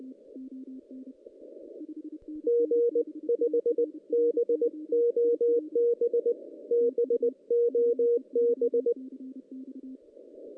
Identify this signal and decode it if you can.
Morse code, heard as a keyed tone, about 19 wpm, 470 Hz. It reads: G 5 BOB BOB
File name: 2016 JAN CW NAQP / 2016 Jan naqp k5ot - 0-6